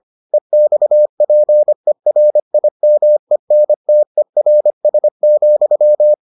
Morse code, heard as a keyed tone, about 25 wpm, 600 Hz. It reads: EXPERIMENTERS,